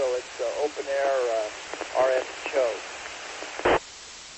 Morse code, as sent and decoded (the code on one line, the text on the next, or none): none